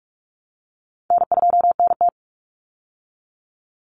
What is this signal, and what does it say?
Morse code, heard as a keyed tone, about 44 wpm, 700 Hz. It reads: N2NT
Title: Easy Call Signs Repeated 3x - 45wpm